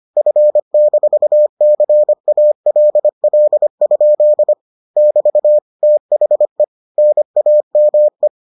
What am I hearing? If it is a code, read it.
Morse code, heard as a keyed tone, about 25 wpm, 600 Hz. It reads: F-CALL? <BT> THE NAME